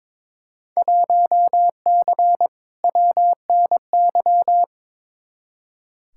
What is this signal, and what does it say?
Morse code, heard as a keyed tone, about 22 wpm, 700 Hz. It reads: 1C WNY
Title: Field Day Exchanges Encore - 22wpm